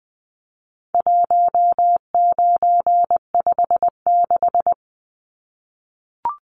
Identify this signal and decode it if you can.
Morse code, heard as a keyed tone, about 20 wpm, 700 Hz. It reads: 1956
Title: Numbers - 4-Digit 20wpm